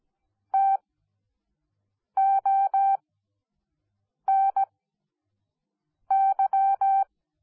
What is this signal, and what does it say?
Morse code, heard as a keyed tone, about 17 wpm, spaced as for 5 wpm, 780 Hz. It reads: TONY